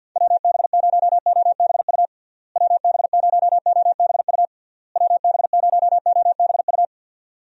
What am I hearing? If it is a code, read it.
Morse code, heard as a keyed tone, about 51 wpm, 700 Hz. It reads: WB0OBU WB0OBU WB0OBU
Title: USA Call Signs - 6 Characters - Repeated 3x - 50wpm